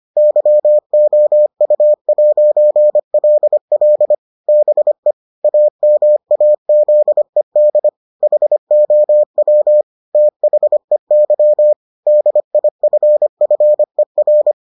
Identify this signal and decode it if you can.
Morse code, heard as a keyed tone, about 25 wpm, 600 Hz. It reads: YOU'LL BE AMAZED HOW THEY DIFFER